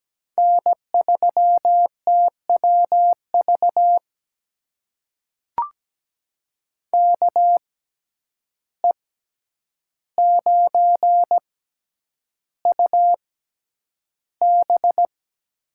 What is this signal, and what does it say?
Morse code, heard as a keyed tone, about 17 wpm, spaced as for 9 wpm, 700 Hz. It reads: N3TWV KE9UB
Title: USA Call Signs - 5 Characters - 6x Character Spacing - 17wpm